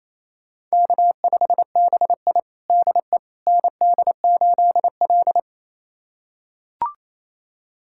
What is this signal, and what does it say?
Morse code, heard as a keyed tone, about 28 wpm, 700 Hz. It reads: K5BI DE ND8L
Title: USA Callsign DE Callsign - 4 Characters - 28wpm